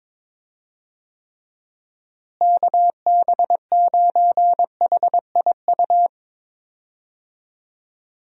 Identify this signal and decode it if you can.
Morse code, heard as a keyed tone, about 22 wpm, 700 Hz. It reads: KB9HIU
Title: USA Call Signs - 6 Characters - Repeated 3x - 22wpm